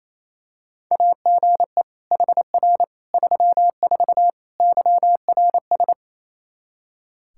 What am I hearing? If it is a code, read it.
Morse code, heard as a keyed tone, about 27 wpm, 700 Hz. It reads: AGE HR 34 YRS
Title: K1USN SST QSO Practice 28wpm